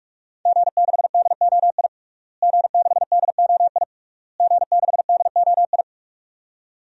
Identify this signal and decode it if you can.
Morse code, heard as a keyed tone, about 41 wpm, 700 Hz. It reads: G6DOI G6DOI G6DOI
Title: DXCC-100 Call Signs - 5 Characters - Repeated 3x - 45wpm